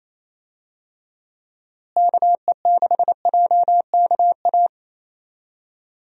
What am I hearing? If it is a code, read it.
Morse code, heard as a keyed tone, about 27 wpm, 700 Hz. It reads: KE6JKA
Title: USA Call Signs - 6 Characters - Repeated 2x - 28wpm